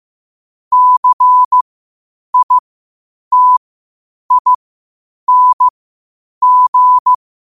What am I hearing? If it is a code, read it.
Morse code, heard as a keyed tone, about 15 wpm, spaced as for 8 wpm, 1000 Hz. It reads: CITING
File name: qotdfiles / 8wpm